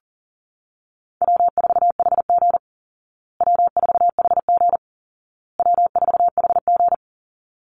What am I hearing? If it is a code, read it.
Morse code, heard as a keyed tone, about 40 wpm, 700 Hz. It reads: W4HG W4HG W4HG
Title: Call Signs Repeated 2x - 40wpm